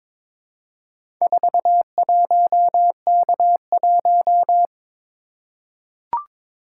Morse code, heard as a keyed tone, about 22 wpm, 700 Hz. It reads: 41K1
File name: practice / 22wpm / Four Letter-Number Combinations - 22wpm